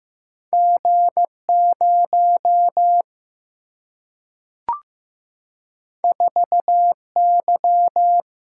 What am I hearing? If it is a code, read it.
Morse code, heard as a keyed tone, about 15 wpm, 700 Hz. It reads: G 0 4 Y
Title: Single-Letter Single-Number 15wpm